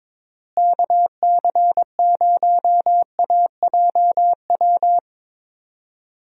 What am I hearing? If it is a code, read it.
Morse code, heard as a keyed tone, about 22 wpm, 700 Hz. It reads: KC0AJW
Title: USA Call Signs - 6 Characters - Repeated 3x - 22wpm